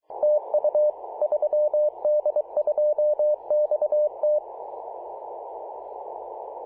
Morse code, heard as a keyed tone, about 23 wpm, 600 Hz. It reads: TU 3D2XT
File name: posts / apr-13-2119z-3d2xt-21-021mhz-cw